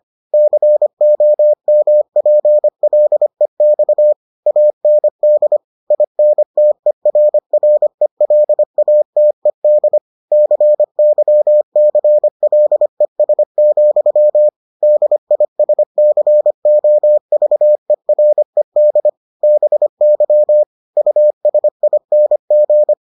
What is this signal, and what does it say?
Morse code, heard as a keyed tone, about 25 wpm, 600 Hz. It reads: COMPLEX AND INTERRELATED CYCLES, DISCOVERED BY USING